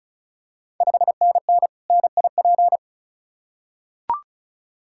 Morse code, heard as a keyed tone, about 34 wpm, 700 Hz. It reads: 5NN NIP
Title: Ontario QSO Party Exchange 35wpm